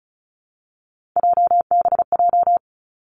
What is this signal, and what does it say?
Morse code, heard as a keyed tone, about 36 wpm, 700 Hz. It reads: JBJ
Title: Three Challenging Letter-Number Combinations 35wpm